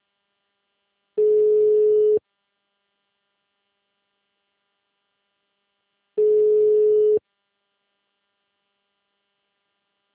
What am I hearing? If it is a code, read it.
Ringback tone (Europe, 425 Hz).